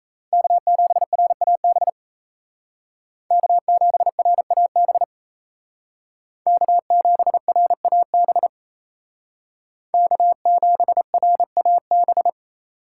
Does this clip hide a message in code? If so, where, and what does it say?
Yes: Morse code from 0.3 to 12.3 s, heard as a keyed tone, about 36 wpm, 700 Hz. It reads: K7RAB K7RAB K7RAB K7RAB